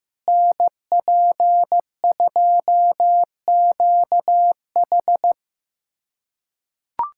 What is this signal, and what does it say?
Morse code, heard as a keyed tone, about 15 wpm, 700 Hz. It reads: NP2QH
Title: USA Callsign DE Callsign - 5 Characters - 15wpm